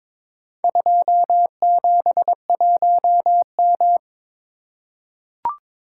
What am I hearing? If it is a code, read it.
Morse code, heard as a keyed tone, about 22 wpm, 700 Hz. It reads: 271M
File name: practice / 22wpm / Four Letter-Number Combinations - 22wpm